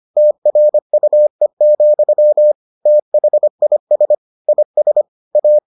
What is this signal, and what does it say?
Morse code, heard as a keyed tone, about 25 wpm, 600 Hz. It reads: TRUE, THIS IS A